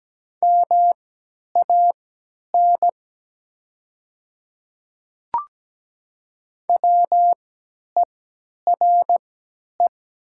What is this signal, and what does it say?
Morse code, heard as a keyed tone, about 17 wpm, spaced as for 10 wpm, 700 Hz. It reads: MAN WERE
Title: 3x Character Spacing - Top 500 Words - 17wpm